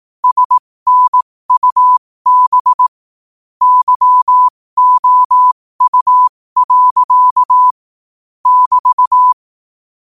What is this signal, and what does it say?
Morse code, heard as a keyed tone, about 18 wpm, spaced as for 15 wpm, 1000 Hz. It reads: SNUB YOU. <BT>
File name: qotdfiles / 15wpm